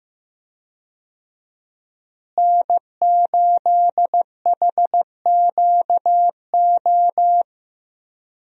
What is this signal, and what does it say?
Morse code, heard as a keyed tone, about 16 wpm, 700 Hz. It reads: N8HQO
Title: USA Call Signs - 5 Characters - Repeated 3x - 15wpm